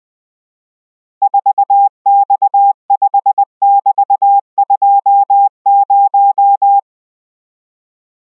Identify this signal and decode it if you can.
Morse code, heard as a keyed tone, about 20 wpm, 800 Hz. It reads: 4X5<BT>20